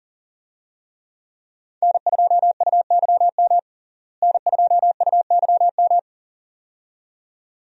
Morse code, heard as a keyed tone, about 40 wpm, 700 Hz. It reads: N2UYM N2UYM
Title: USA Call Signs - 5 Characters - Repeated 2x - 40wpm